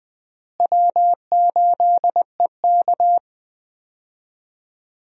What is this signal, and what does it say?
Morse code, heard as a keyed tone, about 20 wpm, 700 Hz. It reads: W8EK
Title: USA Call Signs - 4 Characters - Repeated 2x - 20wpm